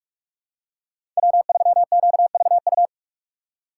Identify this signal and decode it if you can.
Morse code, heard as a keyed tone, about 46 wpm, 700 Hz. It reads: W3QVU
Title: USA Call Signs - 5 Characters - 3x Character Spacing - 45wpm